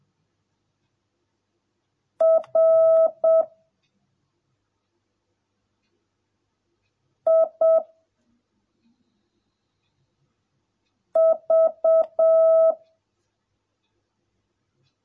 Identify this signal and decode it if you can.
Morse code, heard as a keyed tone, about 7 wpm, 650 Hz. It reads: R I V